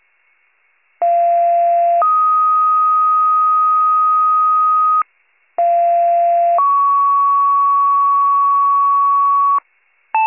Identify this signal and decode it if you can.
Two-tone page, heard as a one-second tone, then a three-second tone, about 690 then 1250 Hz, about 690 then 1120 Hz. It is heard from 1.0 to 9.6 s.